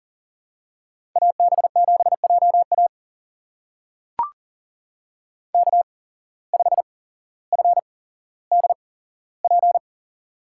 Morse code, heard as a keyed tone, about 40 wpm, spaced as for 17 wpm, 700 Hz. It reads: AB7JA K5FDP
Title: USA Call Signs - 5 Characters - 8x Character Spacing - 40wpm